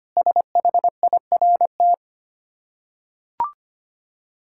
Morse code, heard as a keyed tone, about 25 wpm, 700 Hz. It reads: SHIRT
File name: practice / 25wpm / Difficult Words with Hints - 25wpm